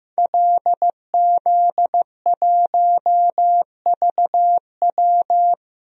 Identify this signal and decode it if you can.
Morse code, heard as a keyed tone, about 15 wpm, 700 Hz. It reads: LZ1VW
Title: DXCC-100 Call Signs - 5 Characters - Repeated 3x - 15wpm